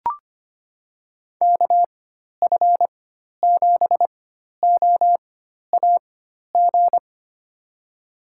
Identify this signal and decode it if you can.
Morse code, heard as a keyed tone, about 25 wpm, spaced as for 12 wpm, 700 Hz. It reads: KF7OAG
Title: USA Call Signs - 6 Characters - 4x Character Spacing - 25wpm